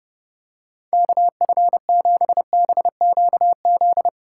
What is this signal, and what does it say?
Morse code, heard as a keyed tone, about 30 wpm, 700 Hz. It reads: KF7BQZ
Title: USA Callsign DE Callsign - 6 Characters - 30wpm